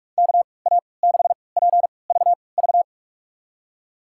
Morse code, heard as a keyed tone, about 47 wpm, spaced as for 26 wpm, 700 Hz. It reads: KA6PVV
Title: USA Call Signs - 6 Characters - 3x Character Spacing - 45wpm